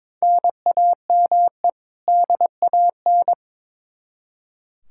Morse code, heard as a keyed tone, about 22 wpm, 700 Hz. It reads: NAME DAN
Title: Practice QSO - Exchange 1 - Beginning - 22wpm